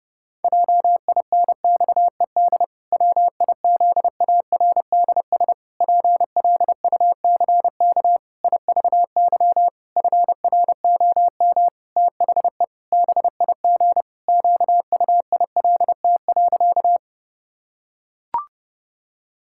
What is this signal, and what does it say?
Morse code, heard as a keyed tone, about 30 wpm, 700 Hz. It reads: JINXED WIZARDS PLUCK IVY FROM THE BIG QUILT.
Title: Pangram Challenge v2 - 30wpm